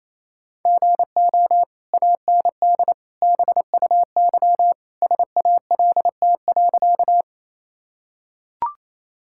Morse code, heard as a keyed tone, about 28 wpm, 700 Hz. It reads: GO AND BUY SALT.
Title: Sentences from Top 1000 Words - Review Part 5 - 28wpm